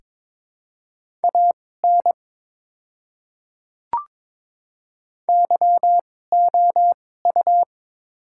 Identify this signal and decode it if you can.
Morse code, heard as a keyed tone, about 22 wpm, spaced as for 16 wpm, 690 Hz. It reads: AN YOU
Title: 2x Character Spacing - Top 100 Words - 22wpm